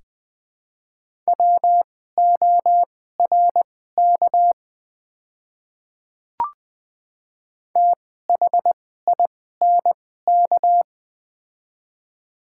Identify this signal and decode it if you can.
Morse code, heard as a keyed tone, about 20 wpm, spaced as for 14 wpm, 700 Hz. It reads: WORK THINK
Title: 2x Character Spacing - Top 500 Words - 20wpm